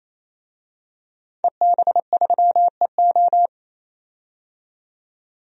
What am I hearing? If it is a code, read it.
Morse code, heard as a keyed tone, about 28 wpm, 700 Hz. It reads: EB3EO
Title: DXCC-100 Call Signs - 5 Characters - 28wpm